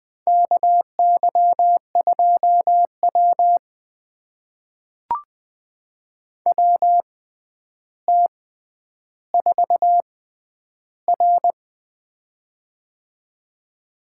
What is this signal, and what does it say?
Morse code, heard as a keyed tone, about 21 wpm, spaced as for 10 wpm, 700 Hz. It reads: KY2W WT4R
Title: USA Call Signs - 4 Characters - 6x Character Spacing - 20wpm